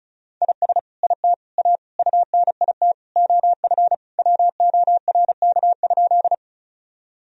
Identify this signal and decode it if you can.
Morse code, heard as a keyed tone, about 36 wpm, 700 Hz. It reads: IS IT A UNIT OF WORK?
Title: Sentences from Top 1000 Words - Review Part 2 - 35wpm